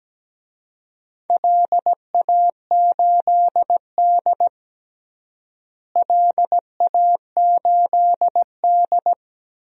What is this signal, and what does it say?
Morse code, heard as a keyed tone, about 17 wpm, 700 Hz. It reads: LA8D LA8D
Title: DXCC-100 Call Signs - 4 Characters - Repeated 2x - 17wpm